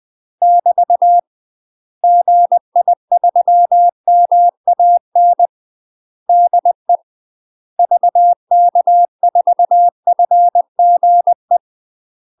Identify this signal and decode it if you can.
Morse code, heard as a keyed tone, about 20 wpm, 700 Hz. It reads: <BT> GI3MAN DE VK4FGE
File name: UK MASTER QSOs TRAINING / 20WPM / UK MASTER QSOs 0002